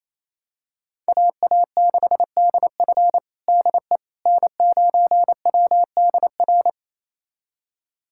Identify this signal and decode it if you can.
Morse code, heard as a keyed tone, about 28 wpm, 700 Hz. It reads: AA6DF DE N9WDR
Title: USA Callsign DE Callsign - 5 Characters - 28wpm